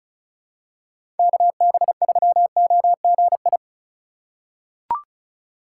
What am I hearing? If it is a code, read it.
Morse code, heard as a keyed tone, about 34 wpm, 700 Hz. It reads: KB3OGI
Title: USA Call Signs - 6 Characters - 3x Character Spacing - 35wpm